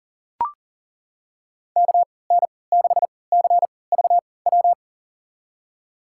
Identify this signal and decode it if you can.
Morse code, heard as a keyed tone, about 40 wpm, spaced as for 23 wpm, 700 Hz. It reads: KN6CVW
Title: USA Call Signs - 3x Character Spacing - 40wpm